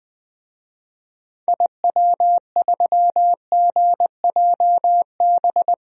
Morse code, heard as a keyed tone, about 21 wpm, 700 Hz. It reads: IW3GJB
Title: DXCC-100 Call Signs - 6 Characters - Repeated 2x - 20wpm